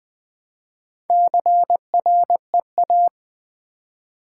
Morse code, heard as a keyed tone, about 20 wpm, 700 Hz. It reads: CREA
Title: Build A Word - 2 Letters At A Time - 6-Letter Words - 20wpm